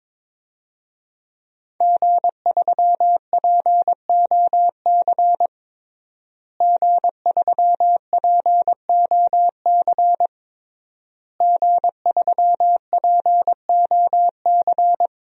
Morse code, heard as a keyed tone, about 22 wpm, 700 Hz. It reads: G3POC G3POC G3POC
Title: DXCC-100 Call Signs - 5 Characters - Repeated 3x - 22wpm